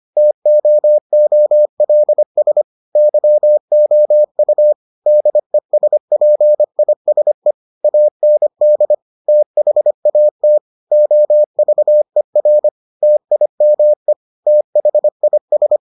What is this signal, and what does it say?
Morse code, heard as a keyed tone, about 25 wpm, 600 Hz. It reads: TOOLS YOU DESPISE AND THAT OVER TIME THIS